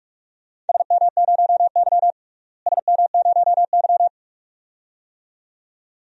Morse code, heard as a keyed tone, about 44 wpm, 700 Hz. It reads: SM0Y SM0Y
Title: DXCC-100 Call Signs - 4 Characters - Repeated 2x - 45wpm